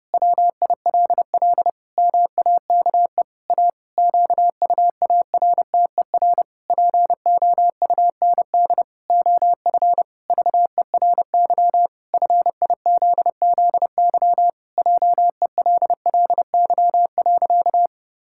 Morse code, heard as a keyed tone, about 30 wpm, 700 Hz. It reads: WILL MAKE A QUARTER POUND OF VERY FIZZY JELLY.
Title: Pangram Challenge v2 - 30wpm